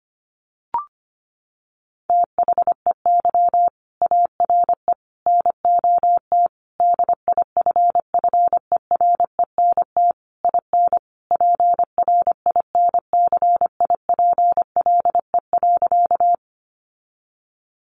Morse code, heard as a keyed tone, about 25 wpm, 700 Hz. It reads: THEY ARE NOT DIFFERENT IN PRINCIPLE.
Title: Sentences from Top 1400 Words 25wpm